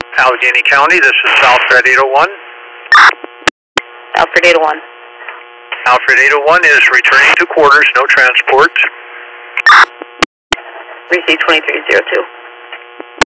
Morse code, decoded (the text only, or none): none